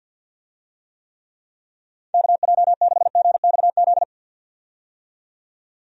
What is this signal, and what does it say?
Morse code, heard as a keyed tone, about 48 wpm, 700 Hz. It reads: KJ6GXZ
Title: USA Call Signs - 6 Characters - 50wpm